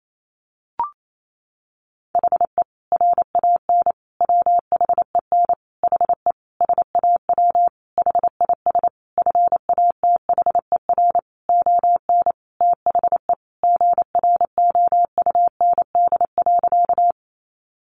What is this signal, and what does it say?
Morse code, heard as a keyed tone, about 28 wpm, 700 Hz. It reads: HE RAN WHEN HE SAW HIS FATHER ON THE GROUND.
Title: Sentences from Top 500 Words 28wpm